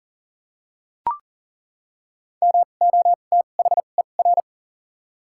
Morse code, heard as a keyed tone, about 40 wpm, spaced as for 29 wpm, 690 Hz. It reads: MOTHER
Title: 2x Character Spacing - Top 500 Words - 40wpm